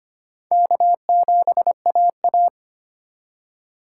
Morse code, heard as a keyed tone, about 32 wpm, 700 Hz. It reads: K7AA